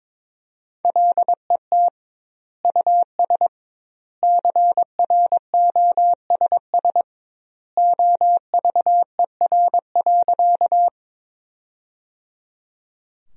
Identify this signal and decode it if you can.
Morse code, heard as a keyed tone, about 22 wpm, 700 Hz. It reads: LET US CROSS OVER.
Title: 2x Word-Spacing - Sentences from Top 300 Words - 22wpm